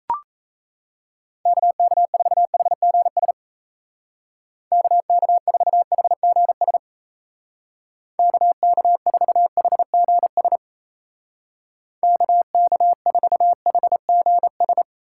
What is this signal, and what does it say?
Morse code, heard as a keyed tone, about 35 wpm, 700 Hz. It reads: KK4HGS KK4HGS KK4HGS KK4HGS
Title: Speed Racing - USA 2x3 Call Signs 28wpm